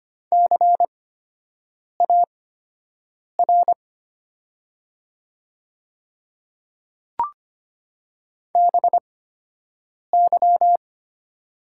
Morse code, heard as a keyed tone, about 25 wpm, spaced as for 7 wpm, 700 Hz. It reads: CAR BY
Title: 8x Character Spacing - Top 500 Words - 25wpm